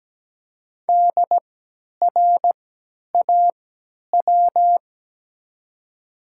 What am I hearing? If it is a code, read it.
Morse code, heard as a keyed tone, about 17 wpm, spaced as for 10 wpm, 700 Hz. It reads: DRAW
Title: 3x Character Spacing - Top 500 Words - 17wpm